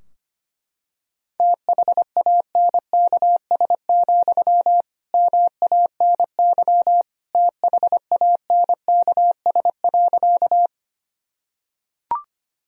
Morse code, heard as a keyed tone, about 25 wpm, 700 Hz. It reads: THANKS, MANY THANKS.
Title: Sentences from Top 1000 Words - Review Part 3 - 25wpm